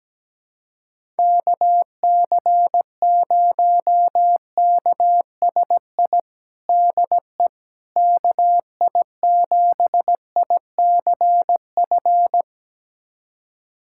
Morse code, heard as a keyed tone, about 17 wpm, 700 Hz. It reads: KC0KSI DE KI7ICF
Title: USA Callsign DE Callsign - 6 Characters - 17wpm